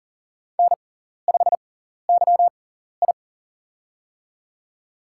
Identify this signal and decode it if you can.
Morse code, heard as a keyed tone, about 37 wpm, spaced as for 14 wpm, 700 Hz. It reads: N5YI
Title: USA Call Signs - 4 Characters - 6x Character Spacing - 40wpm